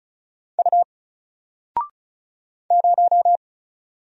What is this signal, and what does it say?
Morse code, heard as a keyed tone, about 34 wpm, 700 Hz. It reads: U 0
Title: Single Letter-Number 35wpm